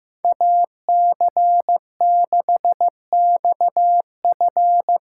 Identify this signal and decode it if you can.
Morse code, heard as a keyed tone, about 15 wpm, 700 Hz. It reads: AC6XF
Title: USA Call Signs - 5 Characters - 4x Character Spacing - 15wpm